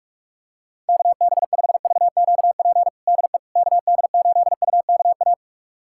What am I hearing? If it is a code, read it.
Morse code, heard as a keyed tone, about 42 wpm, 700 Hz. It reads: KB5VQP DE KD8UKA
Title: USA Callsign DE Callsign - 6 Characters - 45wpm